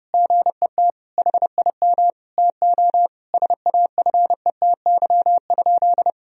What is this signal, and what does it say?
Morse code, heard as a keyed tone, about 30 wpm, 700 Hz. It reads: GET HIM TO SAFETY?
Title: Sentences from Top 1000 Words - Review Part 4 - 30wpm